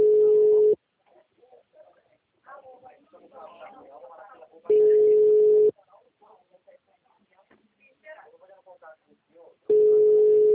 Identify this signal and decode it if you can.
Ringback tone (Europe, 425 Hz), with speech.